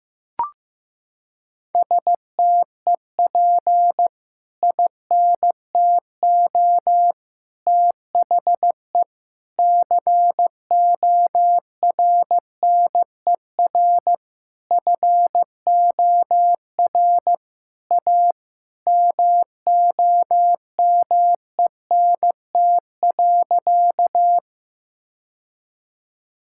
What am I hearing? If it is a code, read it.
Morse code, heard as a keyed tone, about 15 wpm, 700 Hz. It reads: STEP INTO THE CORNER FOR A MOMENT.
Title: Sentences from Top 1600 Words 15wpm